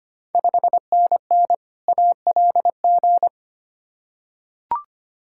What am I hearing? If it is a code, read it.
Morse code, heard as a keyed tone, about 25 wpm, 700 Hz. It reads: 5NN ALG
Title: Ontario QSO Party Exchange 25wpm